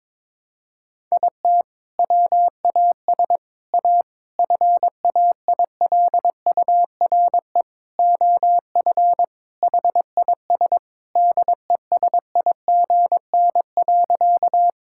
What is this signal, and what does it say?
Morse code, heard as a keyed tone, about 22 wpm, 700 Hz. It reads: IT WAS A FAILURE OF HIS DESIGN.
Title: Sentences from Top 1000 Words - Review Part 6 - 22wpm